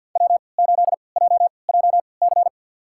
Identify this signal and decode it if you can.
Morse code, heard as a keyed tone, about 47 wpm, spaced as for 28 wpm, 700 Hz. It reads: W8JJC
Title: USA Call Signs - 5 Characters - 3x Character Spacing - 50wpm